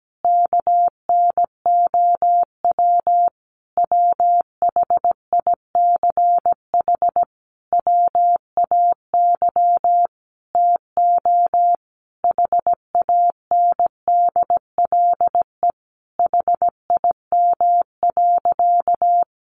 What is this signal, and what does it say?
Morse code, heard as a keyed tone, about 17 wpm, 700 Hz. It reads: KNOW WHICH WAY TO HANDLE HIM.
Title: Sentences from Top 1100 Words 17wpm